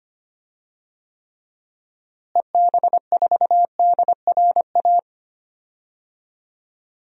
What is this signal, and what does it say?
Morse code, heard as a keyed tone, about 25 wpm, 700 Hz. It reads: EB4DRA
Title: DXCC-100 Call Signs - 6 Characters - Repeated 3x - 25wpm